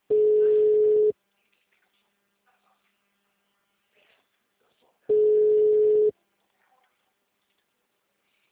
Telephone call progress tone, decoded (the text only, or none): ringback tone